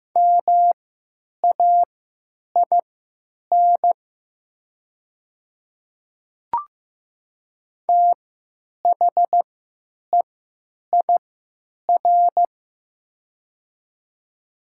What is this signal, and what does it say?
Morse code, heard as a keyed tone, about 15 wpm, spaced as for 9 wpm, 700 Hz. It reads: MAIN THEIR
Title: 3x Character Spacing - Top 300 Words - 15wpm